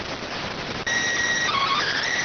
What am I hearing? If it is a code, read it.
SSTV (slow-scan television), PD 290.